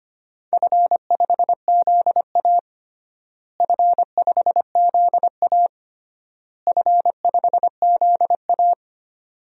Morse code, heard as a keyed tone, about 25 wpm, 700 Hz. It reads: F5ZA F5ZA F5ZA
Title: DXCC-100 Call Signs - 4 Characters - Repeated 3x - 25wpm